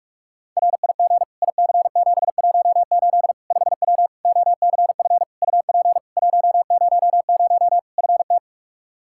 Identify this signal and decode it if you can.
Morse code, heard as a keyed tone, about 42 wpm, 700 Hz. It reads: RIG IC718 5W OCF UP 100 FT
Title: K1USN SST QSO Practice 45wpm